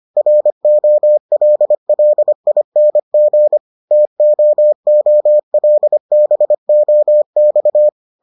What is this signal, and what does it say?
Morse code, heard as a keyed tone, about 25 wpm, 600 Hz. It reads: ROLLING TOOLBOX